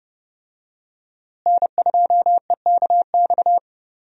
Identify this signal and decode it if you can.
Morse code, heard as a keyed tone, about 31 wpm, 700 Hz. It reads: N2EKX
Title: USA Call Signs - 5 Characters - Repeated 2x - 30wpm